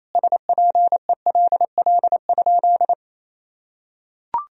Morse code, heard as a keyed tone, about 28 wpm, 700 Hz. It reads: SPELL?
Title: Sentences from Top 1000 Words - Review Part 1 - 28wpm